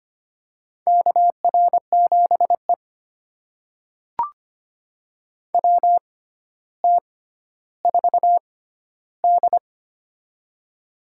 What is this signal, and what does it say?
Morse code, heard as a keyed tone, about 25 wpm, spaced as for 13 wpm, 700 Hz. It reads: KR7E WT4D
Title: USA Call Signs - 4 Characters - 6x Character Spacing - 25wpm